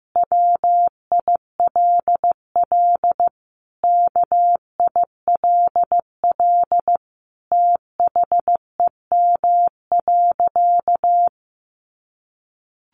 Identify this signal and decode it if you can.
Morse code, heard as a keyed tone, about 15 wpm, 700 Hz. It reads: WILL KILL THEM.
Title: Sentences from Top 1400 Words 15wpm